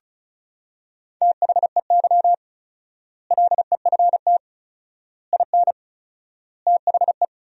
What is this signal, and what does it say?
Morse code, heard as a keyed tone, about 36 wpm, 700 Hz. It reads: THEY LEFT IN THE
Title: 4x Word-Spacing - Sentences from Top 400 Words - 35wpm